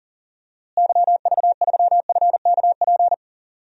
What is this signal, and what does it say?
Morse code, heard as a keyed tone, about 37 wpm, 700 Hz. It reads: YV3FKP
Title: DXCC-100 Call Signs - 6 Characters - Repeated 2x - 40wpm